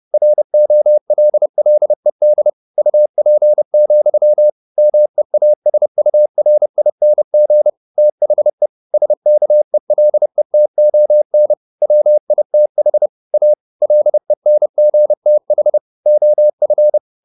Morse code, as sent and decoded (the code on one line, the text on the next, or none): .-. --- .-.. .-.. -.. / ..- .--. --..-- / -- . .- ... ..- .-. .. -. --. / - .... / ... -.- . .-.. - --- -. / .-- .. - .... / .- / .-.. -. --. - .... / --- ..-.
ROLLED UP, MEASURING THE SKELETON WITH A LENGTH OF